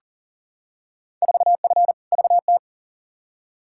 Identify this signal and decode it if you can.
Morse code, heard as a keyed tone, about 40 wpm, 700 Hz. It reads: 4F VT
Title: Field Day Exchanges Encore - 40wpm